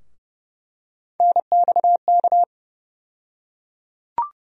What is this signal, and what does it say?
Morse code, heard as a keyed tone, about 30 wpm, 700 Hz. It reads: NXK
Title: USA Call Sign Postfixes - 3 Letters 30wpm